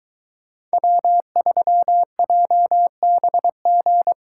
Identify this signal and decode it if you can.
Morse code, heard as a keyed tone, about 22 wpm, 700 Hz. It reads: W3JBG